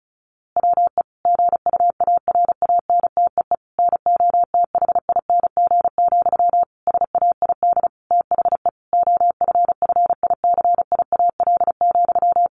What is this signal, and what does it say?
Morse code, heard as a keyed tone, about 34 wpm, 700 Hz. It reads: WE GUARANTEE NOTHING, SAID THE OFFICIAL,
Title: A Sound Of Thunder 35wpm